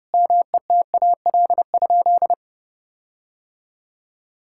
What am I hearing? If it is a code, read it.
Morse code, heard as a keyed tone, about 30 wpm, 700 Hz. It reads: METAL?
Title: Sentences from Top 1600 Words 30wpm